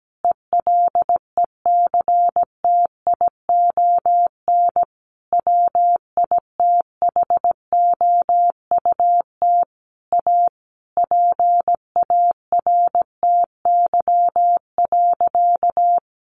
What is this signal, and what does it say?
Morse code, heard as a keyed tone, about 17 wpm, 700 Hz. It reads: ELECTION WITHOUT A PARTY.
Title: Sentences from Top 1500 Words 17wpm